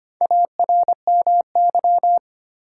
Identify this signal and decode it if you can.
Morse code, heard as a keyed tone, about 25 wpm, 700 Hz. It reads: ARMY
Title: Sentences from Top 800 Words - New Word Drill - 25wpm